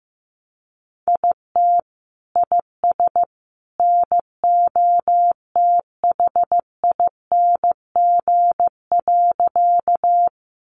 Morse code, heard as a keyed tone, about 15 wpm, 700 Hz. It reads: IT IS NOTHING.
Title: Sentences from Top 500 Words 15wpm